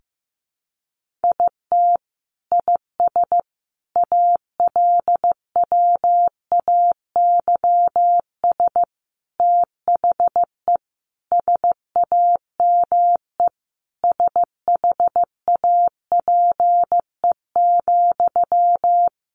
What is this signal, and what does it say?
Morse code, heard as a keyed tone, about 15 wpm, 700 Hz. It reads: IT IS ALWAYS THE SAME SHAPE,
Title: Sentences from Top 1400 Words 15wpm